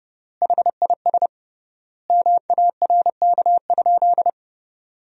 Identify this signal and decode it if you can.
Morse code, heard as a keyed tone, about 30 wpm, 700 Hz. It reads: HIS MARK?
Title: 3x Word-Spacing - Sentences from Top 300 Words - 30wpm